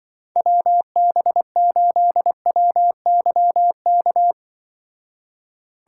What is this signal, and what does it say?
Morse code, heard as a keyed tone, about 24 wpm, 700 Hz. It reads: WB8WYK
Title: Speed Racing - USA 2x3 Call Signs 20wpm